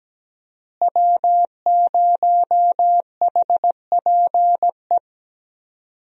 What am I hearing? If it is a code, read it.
Morse code, heard as a keyed tone, about 17 wpm, 700 Hz. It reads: W0HPE
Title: USA Call Signs - 5 Characters - Repeated 3x - 17wpm